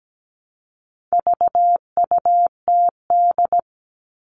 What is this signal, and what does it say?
Morse code, heard as a keyed tone, about 17 wpm, 700 Hz. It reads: VUTD
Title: All Cut Numbers - 4-Digit 17wpm